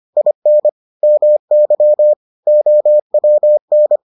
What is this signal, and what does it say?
Morse code, heard as a keyed tone, about 25 wpm, 600 Hz. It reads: IN MY OWN